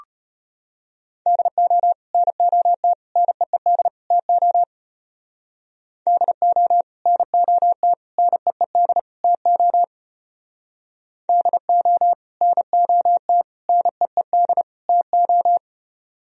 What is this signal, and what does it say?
Morse code, heard as a keyed tone, about 29 wpm, 700 Hz. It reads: DO NOT NEED TO DO NOT NEED TO DO NOT NEED TO
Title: Speed-Racing - Sets of 4 Words 25wpm